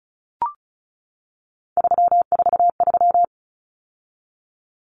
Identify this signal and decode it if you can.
Morse code, heard as a keyed tone, about 34 wpm, 700 Hz. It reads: 343
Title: Common Cut Numbers - 3-Digit 35wpm